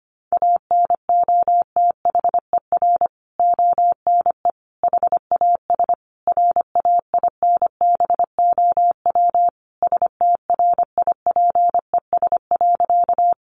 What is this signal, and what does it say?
Morse code, heard as a keyed tone, about 25 wpm, 700 Hz. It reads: ANOTHER ONE HAS RAINBOW STRIPES.
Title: Additional Easy Phrases 25wpm